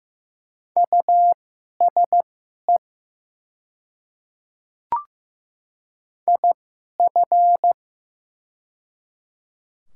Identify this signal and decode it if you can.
Morse code, heard as a keyed tone, about 15 wpm, spaced as for 11 wpm, 700 Hz. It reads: USE IF
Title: 2x Character Spacing - Top 100 Words - 15wpm